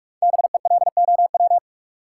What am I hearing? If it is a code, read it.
Morse code, heard as a keyed tone, about 41 wpm, 700 Hz. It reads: BELOW
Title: Sentences from Top 1000 Words - Review Part 5 - New Word Drill - 45wpm